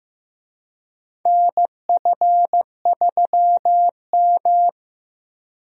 Morse code, heard as a keyed tone, about 15 wpm, 700 Hz. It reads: NF3M
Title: USA Call Signs - 4 Characters - Repeated 2x - 15wpm